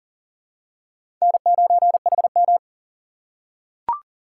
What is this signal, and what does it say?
Morse code, heard as a keyed tone, about 40 wpm, 700 Hz. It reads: N9HM
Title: USA Call Signs - 4 Characters - 4x Character Spacing - 40wpm